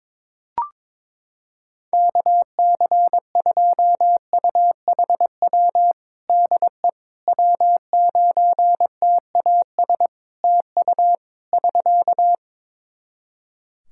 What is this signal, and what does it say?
Morse code, heard as a keyed tone, about 22 wpm, 700 Hz. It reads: KC2UHW DE W9TAS TU <SK>
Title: Practice QSO - Exchange 4 - Ending - 22wpm